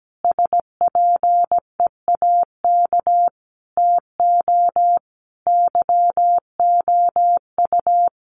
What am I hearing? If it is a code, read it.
Morse code, heard as a keyed tone, about 17 wpm, 700 Hz. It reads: SPEAK TO YOU
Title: Sentences from Top 1200 Words 17wpm